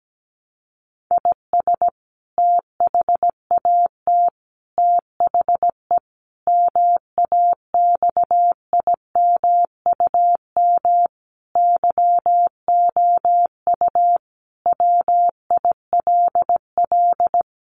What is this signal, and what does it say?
Morse code, heard as a keyed tone, about 17 wpm, 700 Hz. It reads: IS THAT THE MAXIMUM YOU WILL
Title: Sentences from Top 1300 Words 17wpm